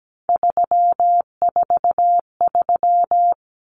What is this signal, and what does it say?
Morse code, heard as a keyed tone, about 17 wpm, 690 Hz. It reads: 343
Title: Common Cut Numbers - 3-Digit 17wpm